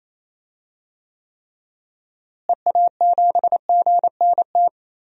Morse code, heard as a keyed tone, about 28 wpm, 700 Hz. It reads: EA7GNT
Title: DXCC-100 Call Signs - 6 Characters - Repeated 3x - 28wpm